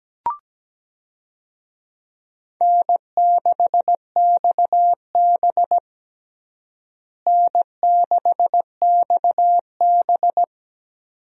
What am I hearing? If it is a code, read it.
Morse code, heard as a keyed tone, about 17 wpm, 700 Hz. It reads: N6XB N6XB
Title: USA Call Signs - 4 Characters - Repeated 2x - 17wpm